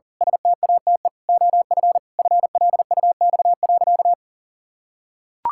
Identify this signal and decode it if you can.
Morse code, heard as a keyed tone, about 37 wpm, 700 Hz. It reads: STATE OF FLUX.
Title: Pangram Challenge v2 - 40wpm